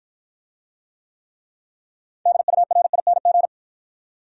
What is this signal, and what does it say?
Morse code, heard as a keyed tone, about 48 wpm, 700 Hz. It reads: DURING